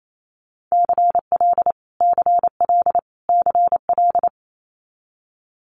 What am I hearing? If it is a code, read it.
Morse code, heard as a keyed tone, about 28 wpm, 700 Hz. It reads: CL CL CL
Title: Multi-letter Phonemes Repeated 3x 28wpm